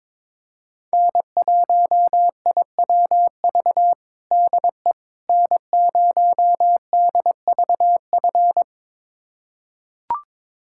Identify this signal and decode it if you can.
Morse code, heard as a keyed tone, about 22 wpm, 700 Hz. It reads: N1IWV DE N0DVF
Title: USA Callsign DE Callsign - 5 Characters - 22wpm